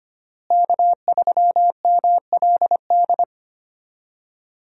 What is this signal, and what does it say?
Morse code, heard as a keyed tone, about 25 wpm, 700 Hz. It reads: K3MLD
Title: USA Call Signs - 5 Characters - Repeated 2x - 25wpm